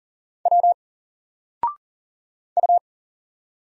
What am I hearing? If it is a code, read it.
Morse code, heard as a keyed tone, about 40 wpm, 700 Hz. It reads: W U
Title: Single Letter-Number 40wpm